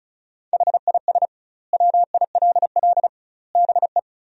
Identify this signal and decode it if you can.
Morse code, heard as a keyed tone, about 36 wpm, 700 Hz. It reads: HIS WILL BE
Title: 2x Word-Spacing - Sentences from Top 500 Words - 35wpm